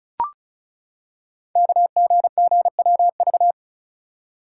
Morse code, heard as a keyed tone, about 36 wpm, 700 Hz. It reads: KGGWV
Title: Code Groups - 5 Letters - 35wpm